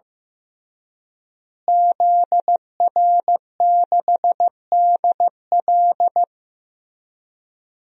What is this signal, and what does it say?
Morse code, heard as a keyed tone, about 15 wpm, 700 Hz. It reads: ZR6DL
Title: DXCC-100 Call Signs - 5 Characters - Repeated 3x - 15wpm